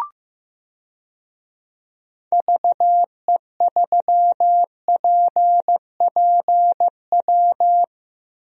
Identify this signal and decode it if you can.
Morse code, heard as a keyed tone, about 15 wpm, 700 Hz. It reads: VE3PPW
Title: DXCC-100 Call Signs - 6 Characters - 15wpm